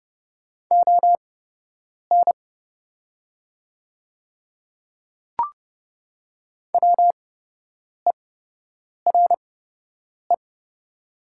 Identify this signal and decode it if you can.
Morse code, heard as a keyed tone, about 31 wpm, spaced as for 8 wpm, 700 Hz. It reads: ON WERE
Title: 8x Character Spacing - Top 100 Words - 30wpm